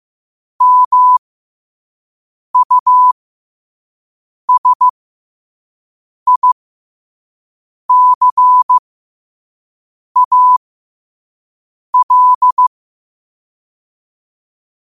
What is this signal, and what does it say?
Morse code, heard as a keyed tone, about 15 wpm, spaced as for 5 wpm, 1000 Hz. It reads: MUSICAL
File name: 5wpm